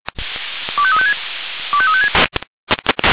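Five-tone selective call (ZVEI), heard as quick runs of five tones, tones of about 70 ms. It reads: 25356 25356
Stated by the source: ZVEI five-tone selective call received on an SDR.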